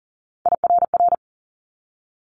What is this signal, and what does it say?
Morse code, heard as a keyed tone, about 38 wpm, 700 Hz. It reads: IRR
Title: Multi-letter Phonemes 40wpm